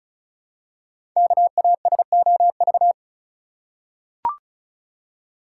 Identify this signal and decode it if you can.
Morse code, heard as a keyed tone, about 36 wpm, 700 Hz. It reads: KASOV
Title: Code Groups - 5 Letters - 35wpm